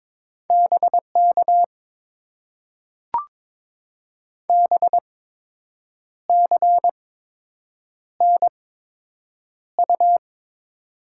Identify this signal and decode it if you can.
Morse code, heard as a keyed tone, about 22 wpm, spaced as for 7 wpm, 690 Hz. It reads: BK BCNU
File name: Top 90 QSO Elements - 8x Character Spacing - 22wpm